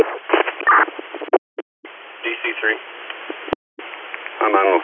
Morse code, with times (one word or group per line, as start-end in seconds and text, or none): none